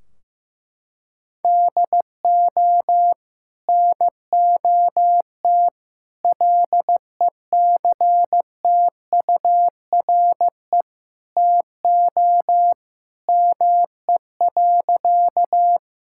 Morse code, heard as a keyed tone, about 15 wpm, 700 Hz. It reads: DO NOT LECTURE TO ME.
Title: Sentences from Top 1600 Words 15wpm